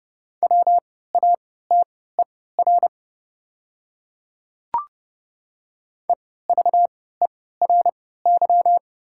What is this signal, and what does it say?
Morse code, heard as a keyed tone, about 30 wpm, spaced as for 17 wpm, 700 Hz. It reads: WATER EVERY
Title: 3x Character Spacing - Top 500 Words - 30wpm